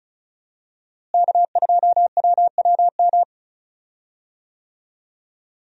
Morse code, heard as a keyed tone, about 34 wpm, 700 Hz. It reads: K2WWM
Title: USA Call Signs - 5 Characters - Repeated 2x - 35wpm